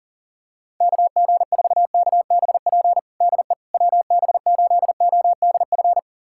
Morse code, heard as a keyed tone, about 40 wpm, 700 Hz. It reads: KG4KBP DE WB8ODF
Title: USA Callsign DE Callsign 40wpm